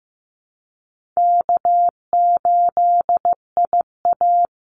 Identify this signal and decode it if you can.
Morse code, heard as a keyed tone, about 15 wpm, 700 Hz. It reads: K8IA
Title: Easy Call Signs - 15wpm